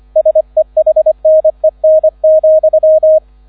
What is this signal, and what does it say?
Morse code, heard as a keyed tone, about 25 wpm, 630 Hz. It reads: SEHNEN,
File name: 001 Erster Streich 25wpm